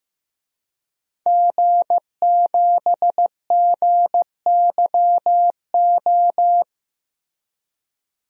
Morse code, heard as a keyed tone, about 15 wpm, 700 Hz. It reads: G7GYO
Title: DXCC-100 Call Signs - 5 Characters - Repeated 2x - 15wpm